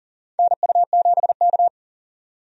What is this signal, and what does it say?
Morse code, heard as a keyed tone, about 42 wpm, 690 Hz. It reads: NU7K
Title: USA Call Signs - 4 Characters - 40wpm